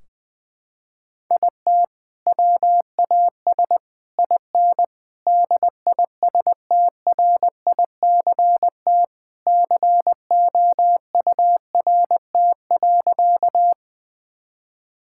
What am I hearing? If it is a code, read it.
Morse code, heard as a keyed tone, about 20 wpm, 700 Hz. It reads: IT WAS IN DISTRICT COURT.
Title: Sentences from Top 1600 Words 20wpm